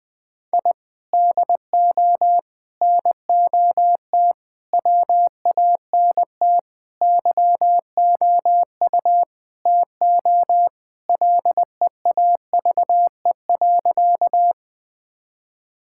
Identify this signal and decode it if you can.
Morse code, heard as a keyed tone, about 20 wpm, 700 Hz. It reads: I DO NOT WANT YOU TO LEAVE.
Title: Sentences with 5-Word Phrases - 20wpm